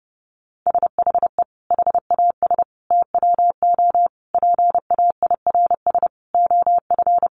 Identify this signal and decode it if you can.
Morse code, heard as a keyed tone, about 30 wpm, 700 Hz. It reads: SHE HAS TWO PAIRS OF
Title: Additional Easy Phrases 30wpm